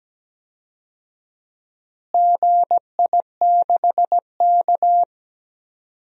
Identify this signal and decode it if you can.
Morse code, heard as a keyed tone, about 17 wpm, 700 Hz. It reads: GI6K
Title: DXCC-100 Call Signs - 4 Characters - Repeated 3x - 17wpm